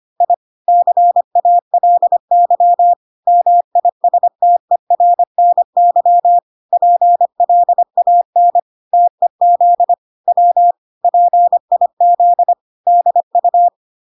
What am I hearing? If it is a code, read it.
Morse code, heard as a keyed tone, about 25 wpm, 700 Hz. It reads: I CALY MISTERNY PLAN TEZ W PIZDU